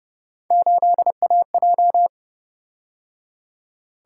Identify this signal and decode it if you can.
Morse code, heard as a keyed tone, about 30 wpm, 700 Hz. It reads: 8AJ
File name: Speed-Racing - Three Letter-Number Combinations - 25wpm